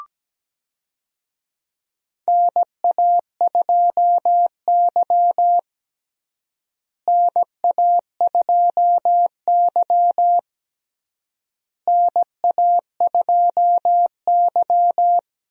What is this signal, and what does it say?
Morse code, heard as a keyed tone, about 17 wpm, 700 Hz. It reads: NA2Y NA2Y NA2Y
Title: USA Call Signs - 4 Characters - Repeated 3x - 17wpm